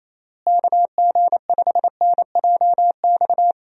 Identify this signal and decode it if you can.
Morse code, heard as a keyed tone, about 28 wpm, 700 Hz. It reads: KG5NJX
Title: USA Call Signs - 6 Characters - Repeated 2x - 28wpm